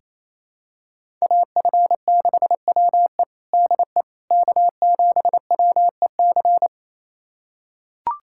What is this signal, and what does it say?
Morse code, heard as a keyed tone, about 28 wpm, 700 Hz. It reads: AF6WE DE K7WEC
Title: USA Callsign DE Callsign - 5 Characters - 28wpm